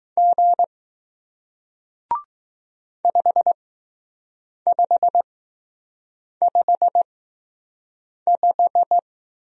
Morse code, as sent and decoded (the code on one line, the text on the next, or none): --. / ..... ..... ..... .....
G 5555